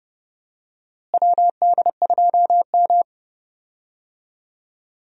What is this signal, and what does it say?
Morse code, heard as a keyed tone, about 30 wpm, 700 Hz. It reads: WD2M
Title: USA Call Signs - 4 Characters - Repeated 3x - 30wpm